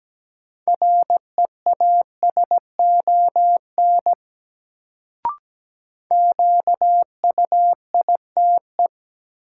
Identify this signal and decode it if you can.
Morse code, heard as a keyed tone, about 17 wpm, 700 Hz. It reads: REASON QUITE
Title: Sentences from Top 700 Words - New Word Drill - 17wpm